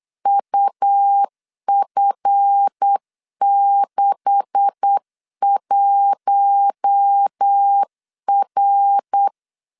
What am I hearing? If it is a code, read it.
Morse code, heard as a keyed tone, about 8 wpm, 800 Hz. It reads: UF61R
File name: LEZIONI / 9 286 8wpm 3425